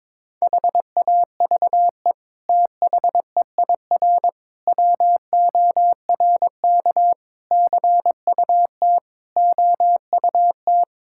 Morse code, heard as a keyed tone, about 22 wpm, 700 Hz. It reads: HAVE THEIR WORK CUT OUT
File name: practice / 22wpm / Sets of 7 Words - 22wpm